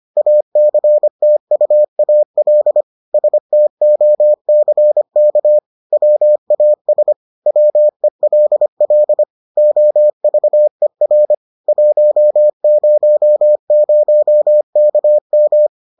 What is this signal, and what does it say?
Morse code, heard as a keyed tone, about 25 wpm, 600 Hz. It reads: ACTUAL STOCK WAS WELL OVER 100KM